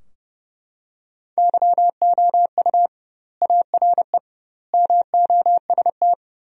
Morse code, heard as a keyed tone, about 32 wpm, 700 Hz. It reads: YOU ARE MOST
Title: 2x Word-Spacing - Sentences from Top 200 Words - 30wpm